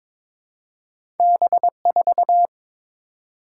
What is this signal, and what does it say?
Morse code, heard as a keyed tone, about 22 wpm, 700 Hz. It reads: B4
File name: practice / 22wpm / Two Letter-Number Combinations - 22wpm